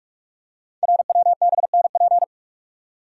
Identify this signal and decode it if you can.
Morse code, heard as a keyed tone, about 44 wpm, 690 Hz. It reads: RWBNP
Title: Code Groups - 5 Letters - 45wpm